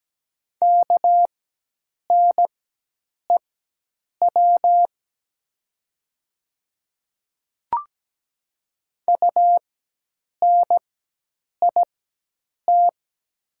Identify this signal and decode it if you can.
Morse code, heard as a keyed tone, about 17 wpm, spaced as for 8 wpm, 700 Hz. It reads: KNEW UNIT
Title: 4x Character Spacing - Top 500 Words - 17wpm